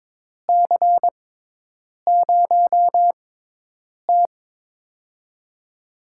Morse code, heard as a keyed tone, about 22 wpm, spaced as for 8 wpm, 700 Hz. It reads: C0T
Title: USA Call Signs - 6x Character Spacing - 22wpm